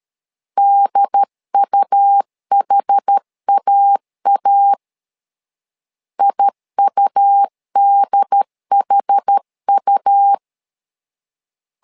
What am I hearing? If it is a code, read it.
Morse code, heard as a keyed tone, about 13 wpm, 800 Hz. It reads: DUHAA IUDHU
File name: LEZIONI / 2 UHDN / 2 UHD 12wpm 1540